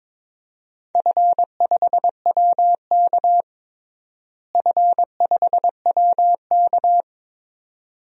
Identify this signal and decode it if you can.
Morse code, heard as a keyed tone, about 22 wpm, 700 Hz. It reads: F5WK F5WK
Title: DXCC-100 Call Signs - 4 Characters - Repeated 2x - 22wpm